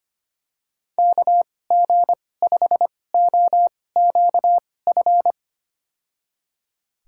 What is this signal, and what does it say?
Morse code, heard as a keyed tone, about 25 wpm, spaced as for 18 wpm, 700 Hz. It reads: KG5OQF
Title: USA Call Signs - 6 Characters - 2x Character Spacing - 25wpm